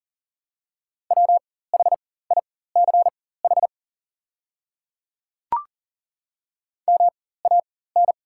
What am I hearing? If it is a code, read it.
Morse code, heard as a keyed tone, about 40 wpm, spaced as for 19 wpm, 700 Hz. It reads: WHICH MAN
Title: 4x Character Spacing - Top 500 Words - 40wpm